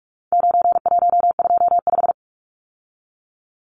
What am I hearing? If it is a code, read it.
Morse code, heard as a keyed tone, about 44 wpm, 700 Hz. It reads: 9125
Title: Numbers - 4-Digit 45wpm